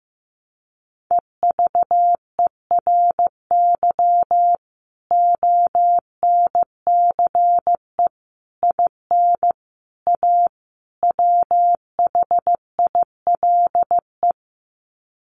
Morse code, heard as a keyed tone, about 15 wpm, 700 Hz. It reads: EVERY ONCE IN A WHILE
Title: Sets of 5 Words 15wpm